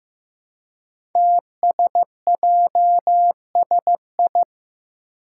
Morse code, heard as a keyed tone, about 15 wpm, 700 Hz. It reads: TSJSI
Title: Five Challenging Letter-Number Combinations 15wpm